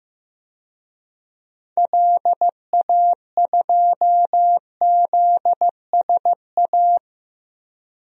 Morse code, heard as a keyed tone, about 16 wpm, 700 Hz. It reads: LA2ZSA
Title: DXCC-100 Call Signs - 6 Characters - Repeated 2x - 15wpm